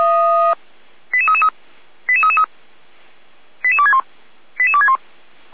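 Five-tone selective call (ZVEI), heard as quick runs of five tones, tones of about 70 ms. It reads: 88333 88333 88261 88261
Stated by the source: ZVEI five-tone call 88190 received by SDR.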